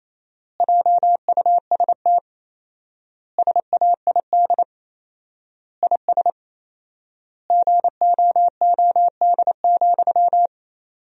Morse code, heard as a keyed tone, about 28 wpm, 700 Hz. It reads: JUST SAID IS GOOD,
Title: 4x Word-Spacing - Sentences from Top 200 Words - 28wpm